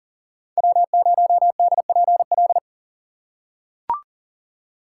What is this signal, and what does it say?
Morse code, heard as a keyed tone, about 40 wpm, 700 Hz. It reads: W0DPL
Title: USA Call Signs - 5 Characters - 6x Character Spacing - 40wpm